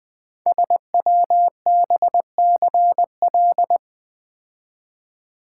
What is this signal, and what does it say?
Morse code, heard as a keyed tone, about 20 wpm, 700 Hz. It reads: SWBCL
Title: Code Groups - 5 Letters - 20wpm